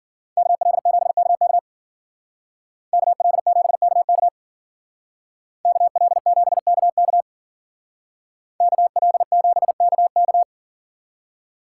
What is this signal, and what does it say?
Morse code, heard as a keyed tone, about 49 wpm, 700 Hz. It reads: KL7KK KL7KK KL7KK KL7KK
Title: Speed Racing - USA Call Signs 40wpm